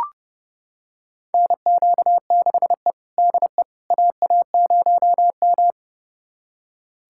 Morse code, heard as a keyed tone, about 30 wpm, 700 Hz. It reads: NQ6E DE AA0M
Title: USA Callsign DE Callsign - 4 Characters - 30wpm